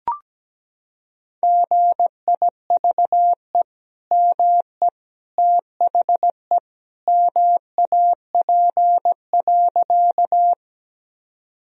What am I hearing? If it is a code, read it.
Morse code, heard as a keyed tone, about 17 wpm, 700 Hz. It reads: GIVE ME THE MAP.
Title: Sentences from Top 1000 Words - Review Part 2 - 17wpm